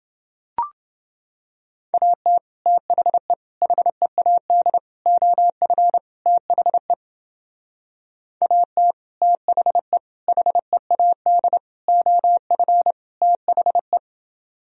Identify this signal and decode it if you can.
Morse code, heard as a keyed tone, about 28 wpm, 700 Hz. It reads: AT THE HEAD OF THE AT THE HEAD OF THE